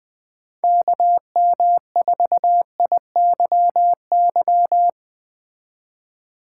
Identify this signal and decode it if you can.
Morse code, heard as a keyed tone, about 20 wpm, 700 Hz. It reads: KM4IYY
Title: Speed Racing - USA Call Signs 17wpm